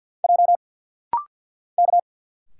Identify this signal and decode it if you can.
Morse code, heard as a keyed tone, about 50 wpm, 700 Hz. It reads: J K
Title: Single Letters - Rapid-Fire 50wpm